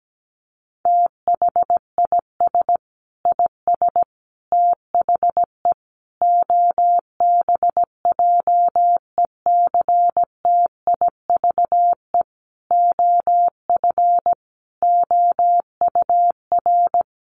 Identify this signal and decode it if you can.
Morse code, heard as a keyed tone, about 17 wpm, 700 Hz. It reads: THIS IS THE OBJECTIVE OF OUR
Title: Sentences from Top 1200 Words 17wpm